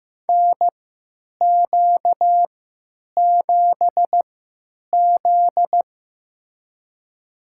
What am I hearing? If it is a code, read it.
Morse code, heard as a keyed tone, about 15 wpm, spaced as for 9 wpm, 700 Hz. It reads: NQ7Z
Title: USA Call Signs - 4 Characters - 3x Character Spacing - 15wpm